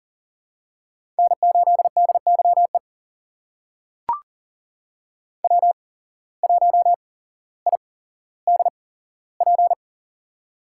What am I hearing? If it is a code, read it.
Morse code, heard as a keyed tone, about 40 wpm, spaced as for 17 wpm, 700 Hz. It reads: N8DYE W1IDP
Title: USA Call Signs - 5 Characters - 8x Character Spacing - 40wpm